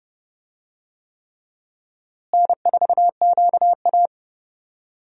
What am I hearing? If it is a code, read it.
Morse code, heard as a keyed tone, about 32 wpm, 700 Hz. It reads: N4QA
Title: USA Call Signs - 4 Characters - Repeated 3x - 30wpm